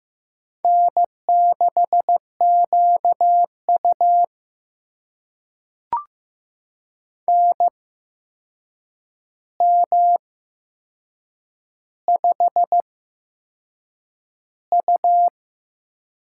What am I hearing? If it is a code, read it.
Morse code, heard as a keyed tone, about 15 wpm, spaced as for 6 wpm, 700 Hz. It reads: N6QU NM5U
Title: USA Call Signs - 4 Characters - 8x Character Spacing - 15wpm